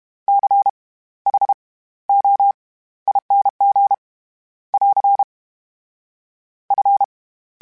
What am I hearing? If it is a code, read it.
Morse code, heard as a keyed tone, about 32 wpm, spaced as for 17 wpm, 800 Hz. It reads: CHOING <AR> F
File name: Special INGs / Special INGs-mp3s / SlowToFaster ing @30